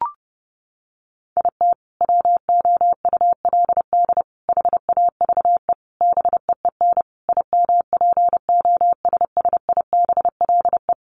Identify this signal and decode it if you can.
Morse code, heard as a keyed tone, about 30 wpm, 700 Hz. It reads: IT WOULD HAVE BEEN IMPOSSIBLE
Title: Sentences from Top 1100 Words 30wpm